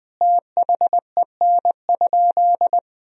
Morse code, heard as a keyed tone, about 20 wpm, 700 Hz. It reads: THEN?
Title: Sentences from Top 500 Words and Abbreviations 20wpm